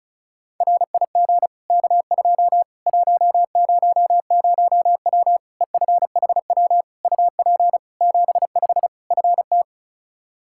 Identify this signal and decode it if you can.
Morse code, heard as a keyed tone, about 34 wpm, 700 Hz. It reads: RIG K2 100W EFHW UP 75 FT
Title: Practice QSO - Exchange 2 - Info - 35wpm